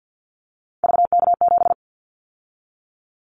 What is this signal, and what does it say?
Morse code, heard as a keyed tone, about 50 wpm, 700 Hz. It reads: VK7
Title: Callsign Prefixes for Top 100 DXCC - 50wpm